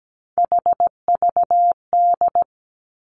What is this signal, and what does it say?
Morse code, heard as a keyed tone, about 17 wpm, 700 Hz. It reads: HVD
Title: Three Challenging Letter-Number Combinations 17wpm